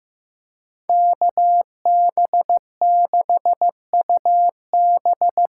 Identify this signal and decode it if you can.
Morse code, heard as a keyed tone, about 15 wpm, 700 Hz. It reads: KB6UB
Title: USA Call Signs - 6 Characters - 8x Character Spacing - 15wpm